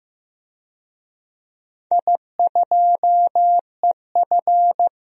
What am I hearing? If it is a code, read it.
Morse code, heard as a keyed tone, about 16 wpm, 700 Hz. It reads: I2EF
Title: DXCC-100 Call Signs - 4 Characters - Repeated 3x - 15wpm